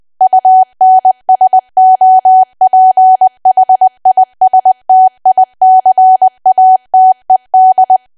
Morse code, heard as a keyed tone, about 20 wpm, 750 Hz. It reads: UNSOPHISTICATED
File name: Archive / 20 WPM / 20WPM